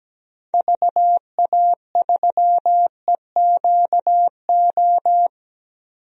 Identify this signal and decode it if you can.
Morse code, heard as a keyed tone, about 17 wpm, 700 Hz. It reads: VA3EQO
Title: DXCC-100 Call Signs - 6 Characters - 17wpm